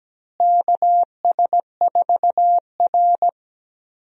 Morse code, heard as a keyed tone, about 17 wpm, 700 Hz. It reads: KS4R
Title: USA Call Signs - 4 Characters - Repeated 3x - 17wpm